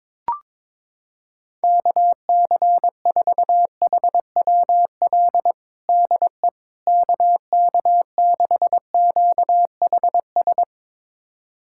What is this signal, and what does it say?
Morse code, heard as a keyed tone, about 22 wpm, 700 Hz. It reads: KC4HWL DE KK6QHS
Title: USA Callsign DE Callsign 22wpm